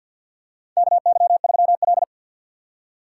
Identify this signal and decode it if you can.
Morse code, heard as a keyed tone, about 50 wpm, 700 Hz. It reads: KY3L
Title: USA Call Signs - 4 Characters - 6x Character Spacing - 50wpm